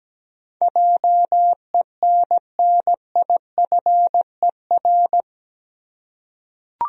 Morse code, heard as a keyed tone, about 17 wpm, 700 Hz. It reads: JENNIFER
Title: USA Common Names - 17wpm